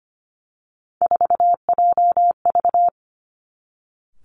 Morse code, heard as a keyed tone, about 25 wpm, 700 Hz. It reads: 4JV
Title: Three Challenging Letter-Number Combinations 25wpm